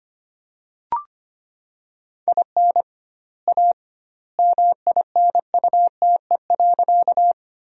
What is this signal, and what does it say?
Morse code, heard as a keyed tone, about 25 wpm, 700 Hz. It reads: IN A MINUTE.
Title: 2x Word-Spacing - Sentences from Top 500 Words - 25wpm